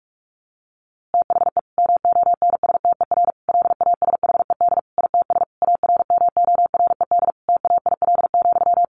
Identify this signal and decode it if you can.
Morse code, heard as a keyed tone, about 44 wpm, 700 Hz. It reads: THE MONSTER LASHED ITS ARMORED TAIL,
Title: A Sound Of Thunder 45wpm